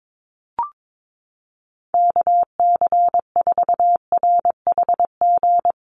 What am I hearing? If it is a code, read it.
Morse code, heard as a keyed tone, about 22 wpm, 700 Hz. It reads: KC4RHG
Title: USA Call Signs Repeated 2x 22wpm